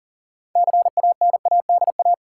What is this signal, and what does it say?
Morse code, heard as a keyed tone, about 40 wpm, 700 Hz. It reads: CANADA
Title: Sets of 5 Words - New Word Drill - 40wpm